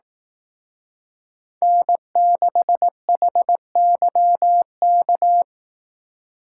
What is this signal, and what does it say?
Morse code, heard as a keyed tone, about 18 wpm, 700 Hz. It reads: N6HYK
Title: Speed Racing - USA 1x3 Call Signs 15wpm